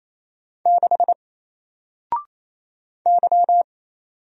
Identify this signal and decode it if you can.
Morse code, heard as a keyed tone, about 29 wpm, 700 Hz. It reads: 6 Y
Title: Single Letter-Number 28wpm